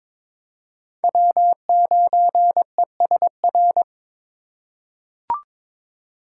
Morse code, heard as a keyed tone, about 22 wpm, 700 Hz. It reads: W9ESR
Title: USA Call Signs - 3x Character Spacing - 22wpm